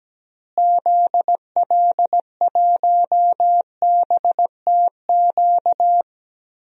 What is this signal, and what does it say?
Morse code, heard as a keyed tone, about 17 wpm, 700 Hz. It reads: ZL1BTQ
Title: DXCC-100 Call Signs - 6 Characters - Repeated 2x - 17wpm